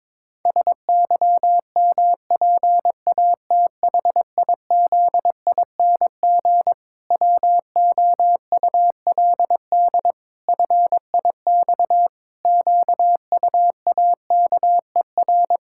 Morse code, heard as a keyed tone, about 22 wpm, 700 Hz. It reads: SYMPATHIZING WOULD FIX QUAKER
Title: Pangram Challenge v2 - 22wpm